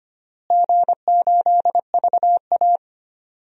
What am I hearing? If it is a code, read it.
Morse code, heard as a keyed tone, about 25 wpm, 700 Hz. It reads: G8VA
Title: DXCC-100 Call Signs - 4 Characters - Repeated 3x - 25wpm